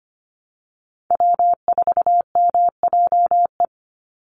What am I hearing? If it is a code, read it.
Morse code, heard as a keyed tone, about 25 wpm, 700 Hz. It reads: W4MJE
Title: Call Signs Repeated 2x - 25wpm